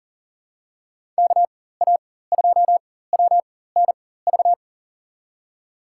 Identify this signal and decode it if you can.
Morse code, heard as a keyed tone, about 40 wpm, spaced as for 19 wpm, 700 Hz. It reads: KA2WNV
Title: USA Call Signs - 6 Characters - 4x Character Spacing - 40wpm